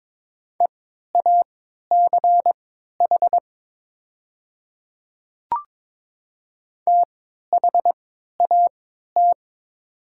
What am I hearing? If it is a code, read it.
Morse code, heard as a keyed tone, about 22 wpm, spaced as for 12 wpm, 700 Hz. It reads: EACH THAT
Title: 3x Character Spacing - Top 100 Words - 22wpm